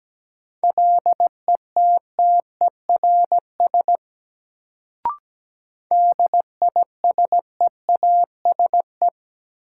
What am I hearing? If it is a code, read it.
Morse code, heard as a keyed tone, about 17 wpm, 700 Hz. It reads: LETTERS DISEASE
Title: Sentences from Top 1000 Words - Review Part 6 - New Word Drill - 17wpm